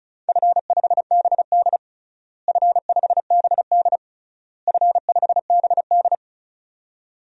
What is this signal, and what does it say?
Morse code, heard as a keyed tone, about 36 wpm, 700 Hz. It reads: F5BD F5BD F5BD
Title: DXCC-100 Call Signs - 4 Characters - Repeated 3x - 35wpm